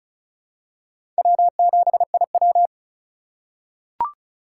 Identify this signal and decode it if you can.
Morse code, heard as a keyed tone, about 34 wpm, 700 Hz. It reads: W7IW
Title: USA Call Signs - 4 Characters - 3x Character Spacing - 35wpm